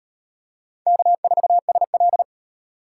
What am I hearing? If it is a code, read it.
Morse code, heard as a keyed tone, about 38 wpm, 700 Hz. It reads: K4SL